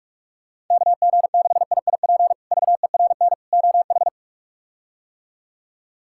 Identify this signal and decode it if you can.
Morse code, heard as a keyed tone, about 42 wpm, 700 Hz. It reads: KG6IIP VERN OH
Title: K1USN SST QSO Practice - 45wpm